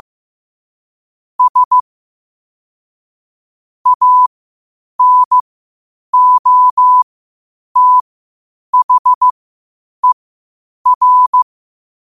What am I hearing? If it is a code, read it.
Morse code, heard as a keyed tone, about 15 wpm, spaced as for 8 wpm, 1000 Hz. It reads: S ANOTHER
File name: qotdfiles / 8wpm